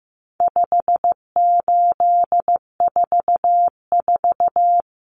Morse code, heard as a keyed tone, about 15 wpm, 700 Hz. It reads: 5844
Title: Common Cut Numbers - 4-Digit 15wpm